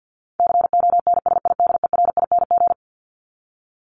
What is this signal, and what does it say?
Morse code, heard as a keyed tone, about 50 wpm, 700 Hz. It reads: CONSIDERING
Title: ING Suffix - Top 100 Words with Base Words 50wpm